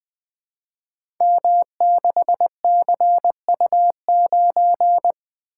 Morse code, heard as a keyed tone, about 20 wpm, 700 Hz. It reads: M6CU9
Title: Code Groups - 5 Letter:Numbers - 20wpm